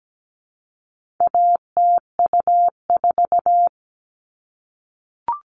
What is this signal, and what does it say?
Morse code, heard as a keyed tone, about 17 wpm, 700 Hz. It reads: ATU4
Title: All Cut Numbers - 4-Digit 17wpm